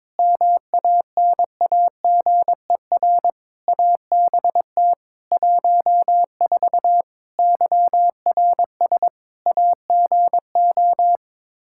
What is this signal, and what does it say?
Morse code, heard as a keyed tone, about 22 wpm, 700 Hz. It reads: MANAGER ABT 14 YRS AGO
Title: Practice QSO - Exchange 3 - More Info - 22wpm